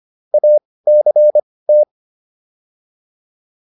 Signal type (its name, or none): Morse code